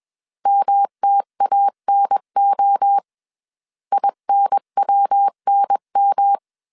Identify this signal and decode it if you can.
Morse code, heard as a keyed tone, about 21 wpm, 800 Hz. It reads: MTANO INWNM
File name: CW / DET / 1 3 1500C S B 20wpm 1754